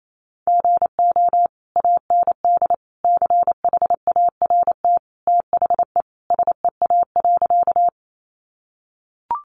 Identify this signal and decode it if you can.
Morse code, heard as a keyed tone, about 28 wpm, 700 Hz. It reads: GO AND CHART THE SEA.
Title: Sentences from Top 1300 Words 28wpm